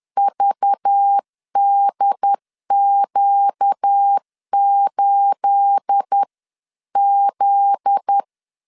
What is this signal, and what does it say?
Morse code, heard as a keyed tone, about 10 wpm, 800 Hz. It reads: VDQ8Z
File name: LEZIONI / 10 SIGNS / 10A SLASH 10wpm 2808